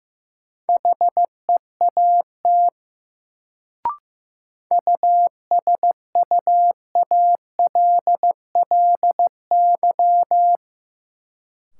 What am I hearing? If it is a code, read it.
Morse code, heard as a keyed tone, about 15 wpm, 700 Hz. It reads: HEAT USUALLY
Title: Sentences from Top 1000 Words - Review Part 3 - New Word Drill - 15wpm